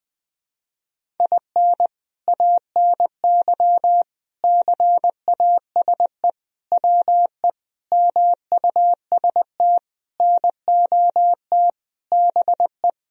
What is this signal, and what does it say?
Morse code, heard as a keyed tone, about 20 wpm, 700 Hz. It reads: IN ANY CASE WE MUST NOT BE
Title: Sentences from Top 1000 Words - Review Part 1 - 20wpm